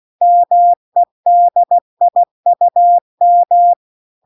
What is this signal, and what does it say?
Morse code, heard as a keyed tone, about 16 wpm, 700 Hz. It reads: MEDIUM
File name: 16WPM